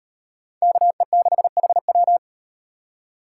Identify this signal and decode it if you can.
Morse code, heard as a keyed tone, about 38 wpm, 700 Hz. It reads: KE6HW